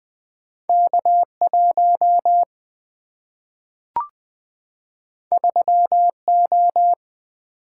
Morse code, heard as a keyed tone, about 20 wpm, 700 Hz. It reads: K 1 3 O
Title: Single-Letter Single-Number 20wpm